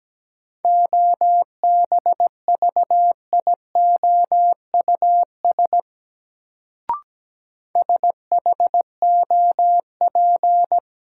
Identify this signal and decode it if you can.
Morse code, heard as a keyed tone, about 17 wpm, 700 Hz. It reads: OBVIOUS SHOP
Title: Sentences from Top 1600 Words - New Word Drill - 17wpm